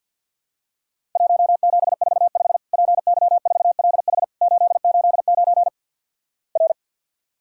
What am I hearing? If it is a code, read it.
Morse code, heard as a keyed tone, about 50 wpm, 700 Hz. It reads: 1745 PY4LH 889 R
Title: Worked All Europe DX Contest - QTC Exchange - 50wpm